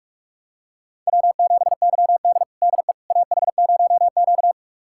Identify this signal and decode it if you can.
Morse code, heard as a keyed tone, about 46 wpm, 700 Hz. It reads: W7YD DE AH0Q
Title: USA Callsign DE Callsign - 4 Characters - 45wpm